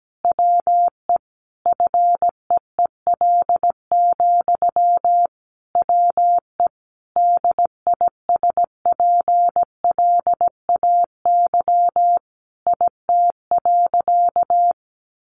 Morse code, heard as a keyed tone, about 17 wpm, 700 Hz. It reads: WE FEEL, WE DISPLAY IT.
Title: Sentences from Top 1200 Words 17wpm